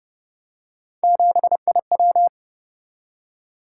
Morse code, heard as a keyed tone, about 31 wpm, 700 Hz. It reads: 7IW